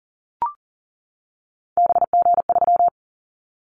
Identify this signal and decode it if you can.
Morse code, heard as a keyed tone, about 39 wpm, 700 Hz. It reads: BG3
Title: Callsign Prefixes for Top 100 DXCC - 40wpm